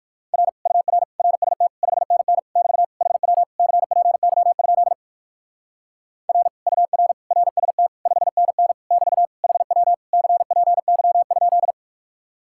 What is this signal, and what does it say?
Morse code, heard as a keyed tone, about 57 wpm, 700 Hz. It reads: R UR RST 5NN <BT> HW CPY? R UR RST 5NN <BT> HW CPY?